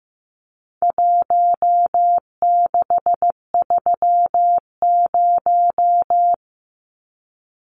Morse code, heard as a keyed tone, about 15 wpm, 700 Hz. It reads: 1630
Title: Numbers - Contest Style - 4-Digit 15wpm